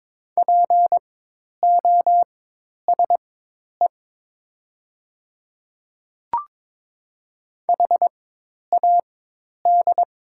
Morse code, heard as a keyed tone, about 22 wpm, spaced as for 10 wpm, 700 Hz. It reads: POSE HAD
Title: 4x Character Spacing - Top 500 Words - 22wpm